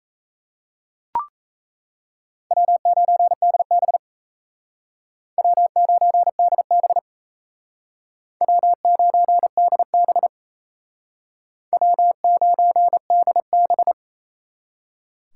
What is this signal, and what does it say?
Morse code, heard as a keyed tone, about 36 wpm, 700 Hz. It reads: W9DB W9DB W9DB W9DB